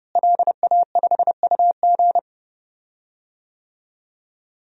Morse code, heard as a keyed tone, about 30 wpm, 700 Hz. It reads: LA5UG
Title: DXCC-100 Call Signs - 5 Characters - Repeated 2x - 30wpm